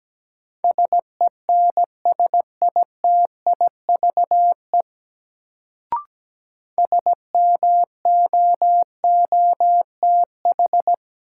Morse code, heard as a keyed tone, about 17 wpm, 700 Hz. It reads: SENSITIVE SMOOTH
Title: Sentences from Top 1400 Words - New Word Drill Alt - 17wpm